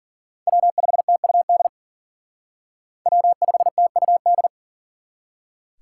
Morse code, heard as a keyed tone, about 50 wpm, 700 Hz. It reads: W5TUD W5TUD
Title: Speed Racing - USA 1x3 Call Signs 40wpm